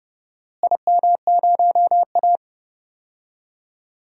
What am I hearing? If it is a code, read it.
Morse code, heard as a keyed tone, about 30 wpm, 700 Hz. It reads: IM0A
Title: DXCC-100 Call Signs - 4 Characters - Repeated 3x - 30wpm